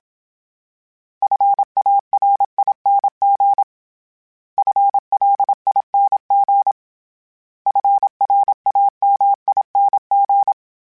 Morse code, heard as a keyed tone, about 27 wpm, 800 Hz. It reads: FARING FLING FRAMING
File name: Special INGs-mp3s / Special INGs-mp3s / Sngl Ing @25